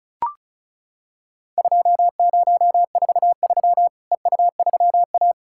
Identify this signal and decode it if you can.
Morse code, heard as a keyed tone, about 36 wpm, 700 Hz. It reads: 2043 EU3A
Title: Worked All Europe DX Contest - QTC Exchange - 35wpm